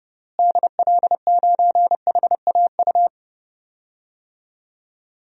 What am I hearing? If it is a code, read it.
Morse code, heard as a keyed tone, about 30 wpm, 700 Hz. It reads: DL9HAU
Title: DXCC-100 Call Signs - 6 Characters - Repeated 3x - 30wpm